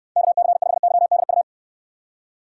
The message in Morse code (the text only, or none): KC5YDU